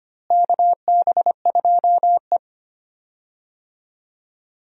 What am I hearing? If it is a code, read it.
Morse code, heard as a keyed tone, about 25 wpm, 700 Hz. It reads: KB2E
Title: USA Call Signs - 4 Characters - Repeated 3x - 25wpm